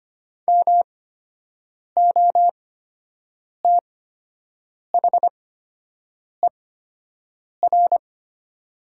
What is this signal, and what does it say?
Morse code, heard as a keyed tone, about 25 wpm, spaced as for 7 wpm, 700 Hz. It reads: MOTHER
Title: 8x Character Spacing - Top 200 Words - 25wpm